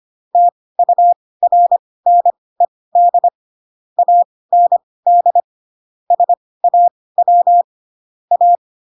Morse code, heard as a keyed tone, about 25 wpm, spaced as for 18 wpm, 700 Hz. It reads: TURNED AND SAW A
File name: Pinocchio / Pinocchio ch4 12 25 ik1hge